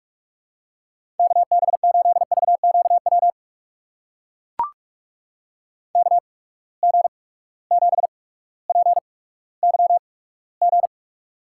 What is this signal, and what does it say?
Morse code, heard as a keyed tone, about 43 wpm, spaced as for 19 wpm, 700 Hz. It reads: KB8VQW KG7PYG
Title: USA Call Signs - 6 Characters - 8x Character Spacing - 45wpm